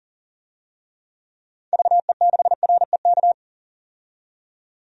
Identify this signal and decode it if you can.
Morse code, heard as a keyed tone, about 40 wpm, 700 Hz. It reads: VE6REK
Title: DXCC-100 Call Signs - 6 Characters - 40wpm